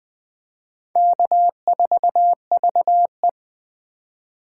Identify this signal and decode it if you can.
Morse code, heard as a keyed tone, about 18 wpm, 700 Hz. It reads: K4VE
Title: Speed Racing - USA 1x2 Call Signs 15wpm